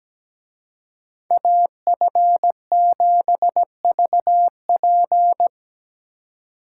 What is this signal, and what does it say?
Morse code, heard as a keyed tone, about 17 wpm, 700 Hz. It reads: AF7VP
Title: Speed Racing - USA 2x2 Call Signs 17wpm